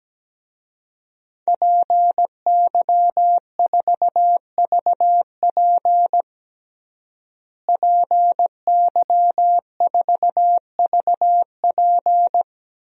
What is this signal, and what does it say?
Morse code, heard as a keyed tone, about 17 wpm, 700 Hz. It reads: PY4VP PY4VP
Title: DXCC-100 Call Signs - 5 Characters - Repeated 3x - 17wpm